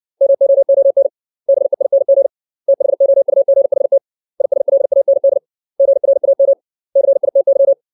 Morse code, heard as a keyed tone, about 60 wpm, 550 Hz. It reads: MOON BEING THOUGHT ISLAND GRAM CITY
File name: 632 words / 632 Words 60WPM